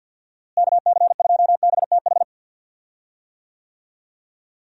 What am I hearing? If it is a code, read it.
Morse code, heard as a keyed tone, about 51 wpm, 700 Hz. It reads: KC2BTH
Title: USA Call Signs - 6 Characters - Repeated 2x - 50wpm